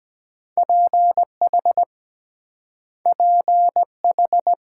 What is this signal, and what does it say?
Morse code, heard as a keyed tone, about 23 wpm, 700 Hz. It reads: PH PH